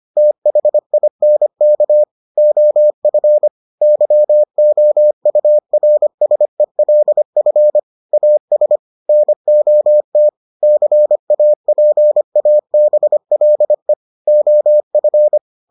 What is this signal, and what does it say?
Morse code, heard as a keyed tone, about 25 wpm, 600 Hz. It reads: THINK OF YOURSELF AS NOT CAPABLE OF